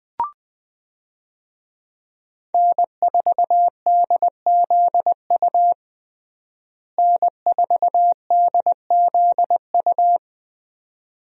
Morse code, heard as a keyed tone, about 20 wpm, 700 Hz. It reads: N4DZU N4DZU
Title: USA Call Signs - 5 Characters - Repeated 2x - 20wpm